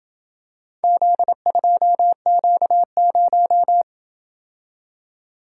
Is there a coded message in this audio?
Yes: Morse code from 0.8 to 3.8 s, heard as a keyed tone, about 26 wpm, 700 Hz. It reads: Z2Q0